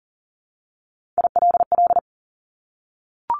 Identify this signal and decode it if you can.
Morse code, heard as a keyed tone, about 40 wpm, 700 Hz. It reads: ILL
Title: Top 1751-2000 Words 40wpm